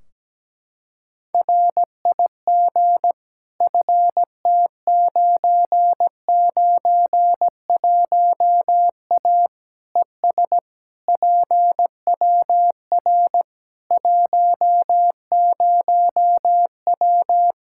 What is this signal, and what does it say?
Morse code, heard as a keyed tone, about 17 wpm, 700 Hz. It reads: RIG FT991A ES PWR 10W
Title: K1USN SST QSO Practice 17wpm